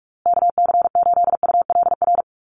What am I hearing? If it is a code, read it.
Morse code, heard as a keyed tone, about 44 wpm, 700 Hz. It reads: KC8ULR
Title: USA Call Signs Repeated 2x 45wpm